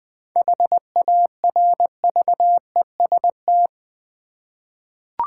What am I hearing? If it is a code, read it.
Morse code, heard as a keyed tone, about 20 wpm, 700 Hz. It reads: HARVEST
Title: 7 Letter Words - First 3 Letters Given - 20wpm